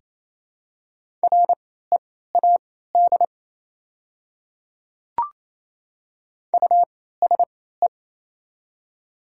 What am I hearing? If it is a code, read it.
Morse code, heard as a keyed tone, about 28 wpm, spaced as for 16 wpm, 700 Hz. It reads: READ USE
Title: 3x Character Spacing - Top 200 Words - 28wpm